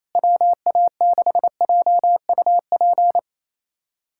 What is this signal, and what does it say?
Morse code, heard as a keyed tone, about 28 wpm, 700 Hz. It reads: WA6JUP
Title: USA Callsign DE Callsign - 6 Characters - 28wpm